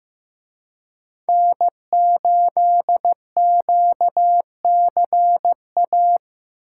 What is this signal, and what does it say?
Morse code, heard as a keyed tone, about 18 wpm, 700 Hz. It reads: N8QCA
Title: Speed Racing - USA 1x3 Call Signs 15wpm